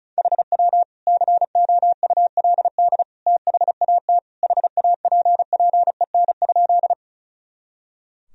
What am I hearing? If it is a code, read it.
Morse code, heard as a keyed tone, about 34 wpm, 700 Hz. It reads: HW COULD THAT HAPPEN?
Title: Sentences from Top 400 Words and Abbreviations 35wpm